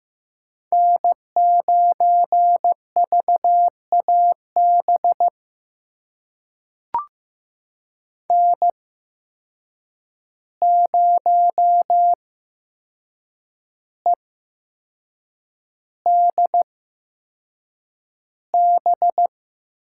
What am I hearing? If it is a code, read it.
Morse code, heard as a keyed tone, about 15 wpm, spaced as for 6 wpm, 700 Hz. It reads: N9VAB N0EDB
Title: USA Call Signs - 5 Characters - 8x Character Spacing - 15wpm